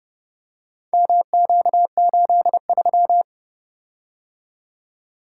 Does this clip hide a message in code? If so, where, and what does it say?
Yes: Morse code from 0.9 to 3.2 s, heard as a keyed tone, about 31 wpm, 700 Hz. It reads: MQ83